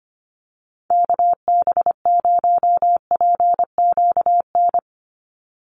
Morse code, heard as a keyed tone, about 25 wpm, 700 Hz. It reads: KB0PQN
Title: USA Call Signs 25wpm